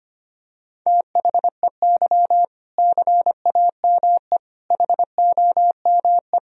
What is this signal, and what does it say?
Morse code, heard as a keyed tone, about 25 wpm, 700 Hz. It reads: THEY CAME HOME
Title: Sentences from Top 1000 Words - Review Part 3 - 25wpm